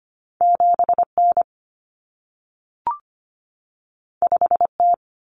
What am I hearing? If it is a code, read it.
Morse code, heard as a keyed tone, about 25 wpm, 700 Hz. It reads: 7N 5T
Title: Common Cut Numbers - 2-Digit 25wpm